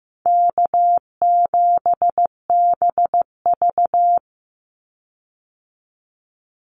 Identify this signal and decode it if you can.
Morse code, heard as a keyed tone, about 15 wpm, 700 Hz. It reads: K7BV
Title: Easy Call Signs - 15wpm